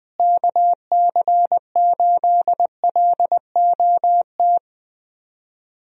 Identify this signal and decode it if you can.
Morse code, heard as a keyed tone, about 20 wpm, 700 Hz. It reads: KC8LOT
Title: USA Call Signs - 6 Characters - Repeated 2x - 20wpm